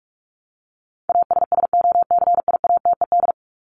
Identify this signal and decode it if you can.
Morse code, heard as a keyed tone, about 44 wpm, 700 Hz. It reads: ASSOCIATED
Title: ED Suffix - Top 100 Words with Base Words 45wpm